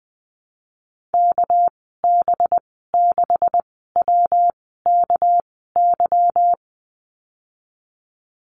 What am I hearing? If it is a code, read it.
Morse code, heard as a keyed tone, about 20 wpm, spaced as for 15 wpm, 700 Hz. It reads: KB6WKY
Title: 207 Lesson 33g - Letter j - Call Signs 20wpm